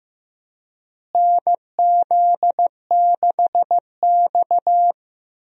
Morse code, heard as a keyed tone, about 15 wpm, 700 Hz. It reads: NZ6X
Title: USA Call Signs - 4 Characters - Repeated 2x - 15wpm